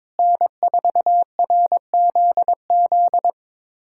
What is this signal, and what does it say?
Morse code, heard as a keyed tone, about 22 wpm, 700 Hz. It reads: N4RZZ
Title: USA Callsign DE Callsign - 5 Characters - 22wpm